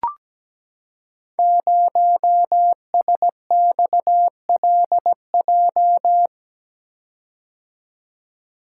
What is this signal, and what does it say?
Morse code, heard as a keyed tone, about 17 wpm, 700 Hz. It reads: 0SXLJ
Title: Code Groups - 5 Letter:Numbers - 17wpm